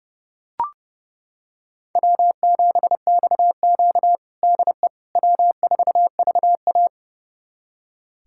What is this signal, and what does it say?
Morse code, heard as a keyed tone, about 30 wpm, 700 Hz. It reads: W7XQ DE W4VA
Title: USA Callsign DE Callsign - 4 Characters - 30wpm